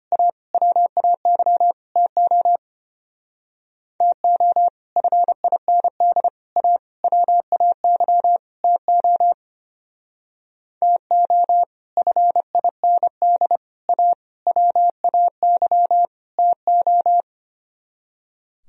Morse code, heard as a keyed tone, about 31 wpm, 700 Hz. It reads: A WAY TO TO FIND A WAY TO TO FIND A WAY TO
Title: Speed-Racing - Sets of 5 Words 25wpm